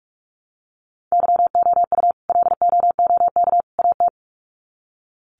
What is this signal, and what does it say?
Morse code, heard as a keyed tone, about 44 wpm, 700 Hz. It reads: YOU LOOK AT
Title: Sets of 3 Words 45wpm